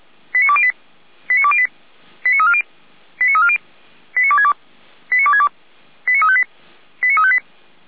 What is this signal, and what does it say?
Five-tone selective call (ZVEI), heard as quick runs of five tones, tones of about 70 ms. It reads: 89208 89208 89340 89340 89262 89262 89358 89358